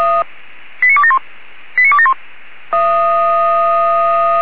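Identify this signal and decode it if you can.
Five-tone selective call (ZVEI), heard as quick runs of five tones, tones of about 70 ms. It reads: 89271 89271
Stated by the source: ZVEI five-tone selective call 89273 received with an SDR.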